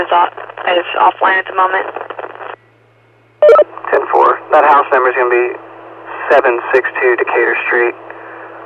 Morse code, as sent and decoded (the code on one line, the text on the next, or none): none